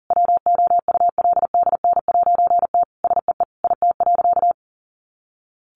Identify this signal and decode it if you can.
Morse code, heard as a keyed tone, about 40 wpm, 700 Hz. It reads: WOULDN'T SEE IT.
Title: A Sound Of Thunder 40wpm